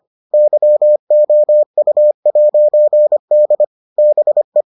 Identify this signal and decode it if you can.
Morse code, heard as a keyed tone, about 25 wpm, 600 Hz. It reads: YOU'D BE